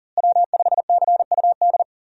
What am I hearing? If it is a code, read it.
Morse code, heard as a keyed tone, about 40 wpm, 700 Hz. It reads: W5CUD
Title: USA Call Signs - 5 Characters - Repeated 2x - 40wpm